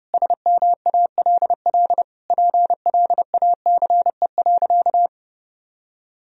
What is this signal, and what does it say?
Morse code, heard as a keyed tone, about 32 wpm, 700 Hz. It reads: SMALL PLACE.
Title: Sentences from Top 1000 Words - Review Part 1 - 30wpm